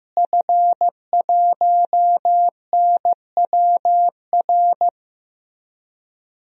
Morse code, heard as a keyed tone, about 15 wpm, 700 Hz. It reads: F1NWR
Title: DXCC-100 Call Signs - 5 Characters - Repeated 2x - 15wpm